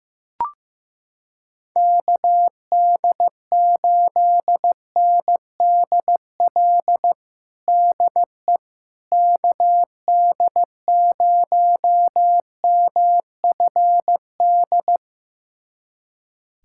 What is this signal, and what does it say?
Morse code, heard as a keyed tone, about 15 wpm, 700 Hz. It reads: KD8NDL DE KD0MFD
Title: USA Callsign DE Callsign 15wpm